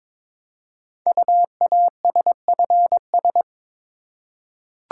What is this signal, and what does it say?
Morse code, heard as a keyed tone, about 22 wpm, 700 Hz. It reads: UASFS
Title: Code Groups - 5 Letters - 22wpm